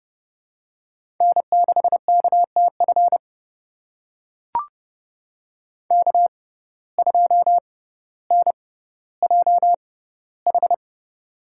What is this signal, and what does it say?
Morse code, heard as a keyed tone, about 30 wpm, spaced as for 15 wpm, 700 Hz. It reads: N6KTF K2NJH
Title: USA Call Signs - 5 Characters - 6x Character Spacing - 30wpm